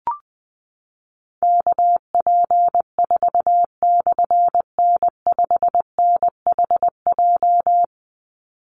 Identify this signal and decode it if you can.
Morse code, heard as a keyed tone, about 20 wpm, 700 Hz. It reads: KP4/N5NHJ
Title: Call Signs Repeated 3x - 20wpm